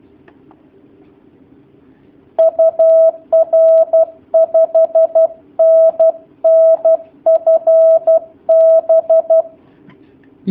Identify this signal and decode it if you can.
Morse code, heard as a keyed tone, about 12 wpm, 650 Hz. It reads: UR 5NN FB
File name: jh8cbh / 2019cw / sp11- w